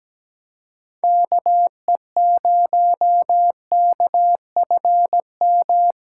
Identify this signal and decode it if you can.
Morse code, heard as a keyed tone, about 17 wpm, 700 Hz. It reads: KE0KFM
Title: USA Call Signs - 3x Character Spacing - 17wpm